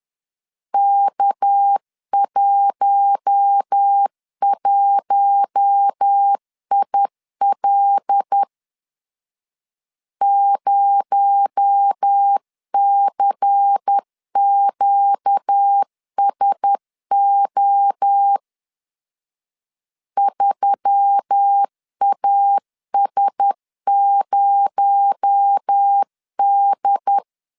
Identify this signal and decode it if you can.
Morse code, heard as a keyed tone, about 10 wpm, 800 Hz. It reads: K11IL 0CQSO 3AS0D
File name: CW / DET / 1 6 1500C S C 10wpm 4252